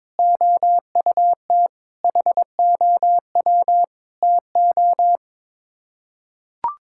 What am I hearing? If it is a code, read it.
Morse code, heard as a keyed tone, about 22 wpm, 700 Hz. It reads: OUT HOW TO
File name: practice / 22wpm / Build A Phrase - 5-Word Phrases - 22wpm